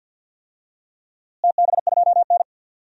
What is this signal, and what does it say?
Morse code, heard as a keyed tone, about 46 wpm, 700 Hz. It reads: TB2N
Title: DXCC-100 Call Signs - 4 Characters - Repeated 3x - 50wpm